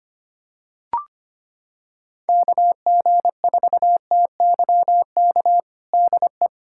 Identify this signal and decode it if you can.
Morse code, heard as a keyed tone, about 25 wpm, 700 Hz. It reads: KG4TYK DE
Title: USA Callsign DE Callsign - 6 Characters - 25wpm